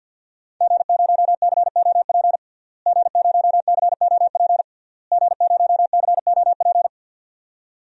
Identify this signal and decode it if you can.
Morse code, heard as a keyed tone, about 46 wpm, 700 Hz. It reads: G0COP G0COP G0COP
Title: DXCC-100 Call Signs - 5 Characters - Repeated 3x - 50wpm